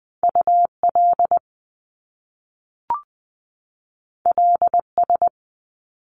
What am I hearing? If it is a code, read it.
Morse code, heard as a keyed tone, about 20 wpm, 700 Hz. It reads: UL LS
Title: Top Two Letter Combinations in Words 20wpm